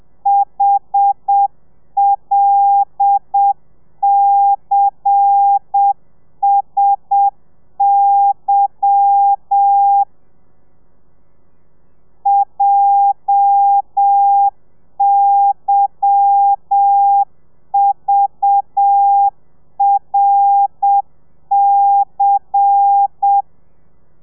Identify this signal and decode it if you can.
Morse code, heard as a keyed tone, about 7 wpm, 800 Hz. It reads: HLCSY JYVRC
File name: telegrafia / i2tfj / Lez10 30